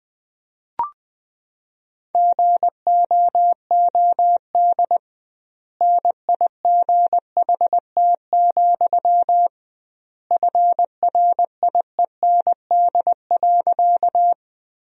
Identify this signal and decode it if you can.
Morse code, heard as a keyed tone, about 21 wpm, 700 Hz. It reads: GOOD NIGHT, FRIEND.
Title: 2x Word-Spacing - Sentences from Top 300 Words - 20wpm